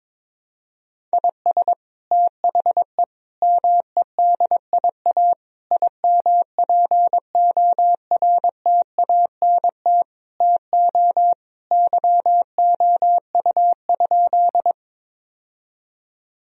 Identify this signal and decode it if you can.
Morse code, heard as a keyed tone, about 22 wpm, 700 Hz. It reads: IS THE MEDIA IMPORTANT TO YOU?
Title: Sentences from Top 1000 Words - Review Part 4 - 22wpm